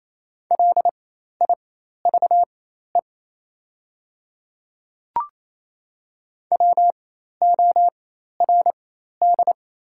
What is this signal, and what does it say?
Morse code, heard as a keyed tone, about 28 wpm, spaced as for 13 wpm, 690 Hz. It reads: LIVE WORD
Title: 4x Character Spacing - Top 200 Words - 28wpm